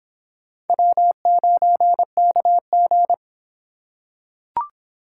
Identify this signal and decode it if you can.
Morse code, heard as a keyed tone, about 25 wpm, 700 Hz. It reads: W9KG